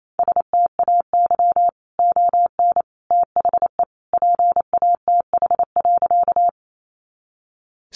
Morse code, heard as a keyed tone, about 28 wpm, 700 Hz. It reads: STAY ON THE PATH.
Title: A Sound Of Thunder 28wpm